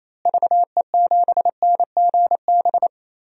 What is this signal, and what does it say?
Morse code, heard as a keyed tone, about 28 wpm, 700 Hz. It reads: VE7NGB
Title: DXCC-100 Call Signs - 6 Characters - Repeated 2x - 28wpm